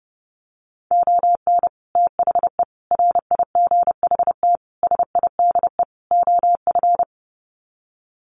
Morse code, heard as a keyed tone, about 30 wpm, 700 Hz. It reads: ON THE RIGHT SIDE OF
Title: Sets of 5 Words Encore 30wpm